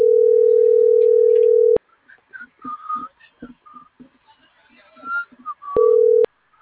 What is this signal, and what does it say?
Ringback tone (North America), 440 + 480 Hz.